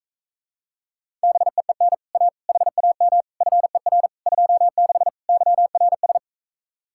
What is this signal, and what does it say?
Morse code, heard as a keyed tone, about 45 wpm, 700 Hz. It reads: BEEN A HAM FER 26 YRS